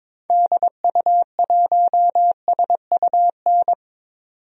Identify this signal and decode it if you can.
Morse code, heard as a keyed tone, about 22 wpm, 700 Hz. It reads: DU1SUN
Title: DXCC-100 Call Signs - 6 Characters - 22wpm